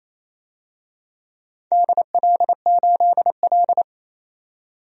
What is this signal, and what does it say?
Morse code, heard as a keyed tone, about 29 wpm, 700 Hz. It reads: DL8L
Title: DXCC-100 Call Signs - 4 Characters - 28wpm